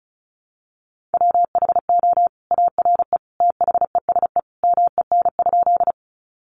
Morse code, heard as a keyed tone, about 34 wpm, 700 Hz. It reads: WHO ARE THESE MEN?
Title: Sentences from Top 200 Words 35wpm